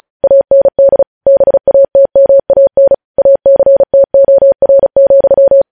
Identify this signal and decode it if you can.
Morse code, heard as a keyed tone, about 36 wpm, 550 Hz. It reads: AND BATMAN ACTOR,
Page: news headlines in morse code at 35 wpm